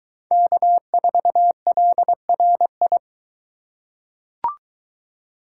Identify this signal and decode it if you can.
Morse code, heard as a keyed tone, about 23 wpm, 700 Hz. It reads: K4LRI